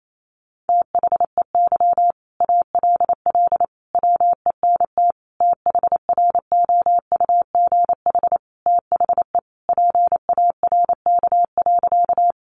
Morse code, heard as a keyed tone, about 28 wpm, 700 Hz. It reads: THEY ALL WENT THROUGH THE PARK.
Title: Sentences from Top 1300 Words 28wpm